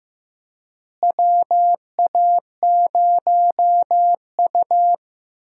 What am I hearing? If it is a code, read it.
Morse code, heard as a keyed tone, about 15 wpm, 700 Hz. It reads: WA0U
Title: USA Call Signs - 4 Characters - Repeated 3x - 15wpm